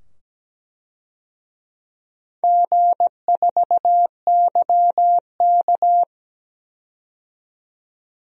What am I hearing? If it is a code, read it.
Morse code, heard as a keyed tone, about 17 wpm, 700 Hz. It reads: G4YK
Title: DXCC-100 Call Signs - 4 Characters - 17wpm